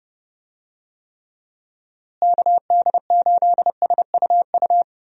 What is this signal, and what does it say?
Morse code, heard as a keyed tone, about 32 wpm, 700 Hz. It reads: KD8SUU
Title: USA Call Signs - 6 Characters - Repeated 2x - 30wpm